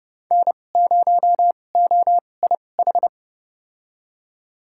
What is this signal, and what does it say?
Morse code, heard as a keyed tone, about 30 wpm, spaced as for 22 wpm, 700 Hz. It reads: N0OIH
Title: USA Call Signs - 5 Characters - 2x Character Spacing - 30wpm